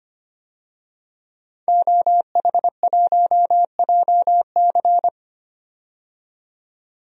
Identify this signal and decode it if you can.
Morse code, heard as a keyed tone, about 25 wpm, 690 Hz. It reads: OH1JC
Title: DXCC-100 Call Signs - 5 Characters - 25wpm